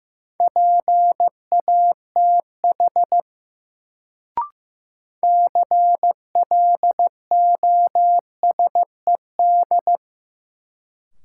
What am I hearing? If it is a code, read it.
Morse code, heard as a keyed tone, about 15 wpm, 700 Hz. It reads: PATH CLOSED
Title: Sentences from Top 1400 Words - New Word Drill - 15wpm